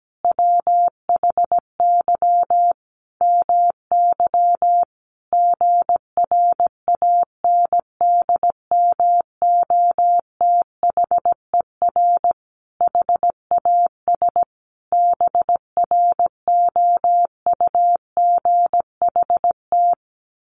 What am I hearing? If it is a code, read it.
Morse code, heard as a keyed tone, about 17 wpm, 700 Hz. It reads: WHY MY GRANDMOTHER HAS BROUGHT
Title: Sentences from Top 1200 Words 17wpm